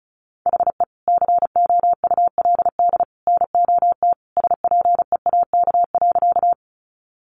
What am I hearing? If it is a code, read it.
Morse code, heard as a keyed tone, about 34 wpm, 700 Hz. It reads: HE COULD NOT SPEAK.
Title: Sentences from Top 1100 Words 35wpm